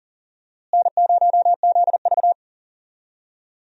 Morse code, heard as a keyed tone, about 40 wpm, 700 Hz. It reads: N0ZV
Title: USA Call Signs - 4 Characters - Repeated 2x - 40wpm